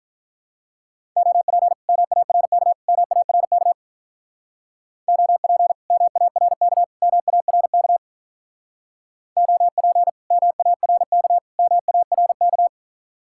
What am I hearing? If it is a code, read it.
Morse code, heard as a keyed tone, about 45 wpm, 700 Hz. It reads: OP MARK MARK OP MARK MARK OP MARK MARK